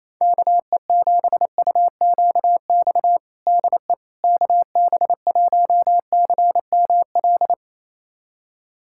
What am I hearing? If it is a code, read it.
Morse code, heard as a keyed tone, about 28 wpm, 700 Hz. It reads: KE7UQX DE KB1CML
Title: USA Callsign DE Callsign 28wpm